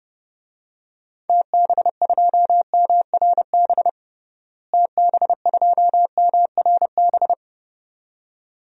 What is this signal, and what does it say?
Morse code, heard as a keyed tone, about 30 wpm, 700 Hz. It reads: TB2MRB TB2MRB
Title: DXCC-100 Call Signs - 6 Characters - Repeated 2x - 30wpm